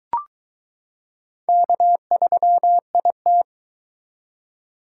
Morse code, heard as a keyed tone, about 22 wpm, 700 Hz. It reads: K3IT